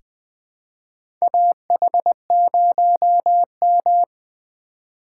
Morse code, heard as a keyed tone, about 20 wpm, 700 Hz. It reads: AH0M
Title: USA Call Signs - 4 Characters - 6x Character Spacing - 20wpm